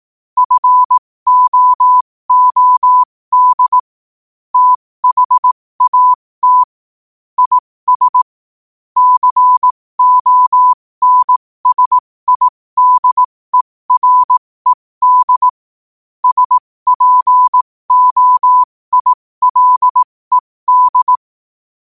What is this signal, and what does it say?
Morse code, heard as a keyed tone, about 18 wpm, spaced as for 15 wpm, 1000 Hz. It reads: FOOD THAT IS CONSIDERED SPOILED
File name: qotdfiles / 15wpm